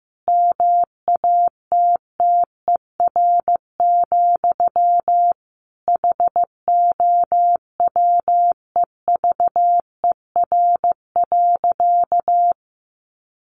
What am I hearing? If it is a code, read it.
Morse code, heard as a keyed tone, about 15 wpm, 700 Hz. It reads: MATTER, HOWEVER.
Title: Sentences from Top 700 Words 15wpm